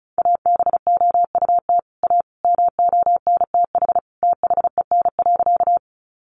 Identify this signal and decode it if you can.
Morse code, heard as a keyed tone, about 34 wpm, 700 Hz. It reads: ABOUT A MONTH THEN.
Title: Sentences from Top 800 Words 35wpm